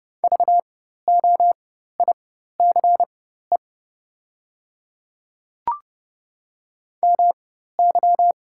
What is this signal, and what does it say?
Morse code, heard as a keyed tone, about 30 wpm, spaced as for 14 wpm, 690 Hz. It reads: VOICE MY
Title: 4x Character Spacing - Top 500 Words - 30wpm